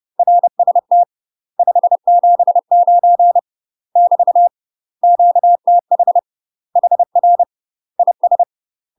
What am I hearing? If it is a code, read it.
Morse code, heard as a keyed tone, about 30 wpm, 700 Hz. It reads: RST 579 <BT> QTH HR IS
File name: UK MASTER QSOs TRAINING / 30WPM / UK MASTER QSOs 0005